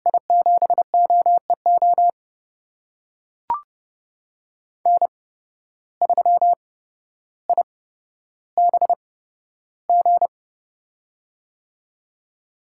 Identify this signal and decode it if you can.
Morse code, heard as a keyed tone, about 32 wpm, spaced as for 13 wpm, 700 Hz. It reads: I7OEO N3IBG
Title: USA Call Signs - 8x Character Spacing - 30wpm